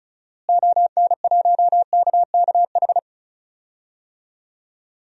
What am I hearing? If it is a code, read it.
Morse code, heard as a keyed tone, about 36 wpm, 700 Hz. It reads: ON1KKH
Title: DXCC-100 Call Signs - 6 Characters - Repeated 2x - 35wpm